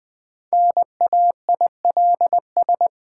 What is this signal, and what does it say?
Morse code, heard as a keyed tone, about 21 wpm, 700 Hz. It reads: NAILS
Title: Difficult Words with Hints - 20wpm